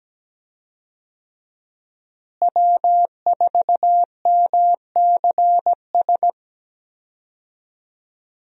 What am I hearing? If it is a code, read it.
Morse code, heard as a keyed tone, about 17 wpm, 700 Hz. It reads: W4MCS
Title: USA Call Signs - 5 Characters - Repeated 3x - 17wpm